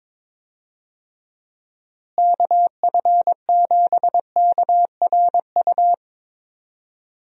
Morse code, heard as a keyed tone, about 22 wpm, 700 Hz. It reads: KF7KRU
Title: USA Call Signs - 6 Characters - Repeated 3x - 22wpm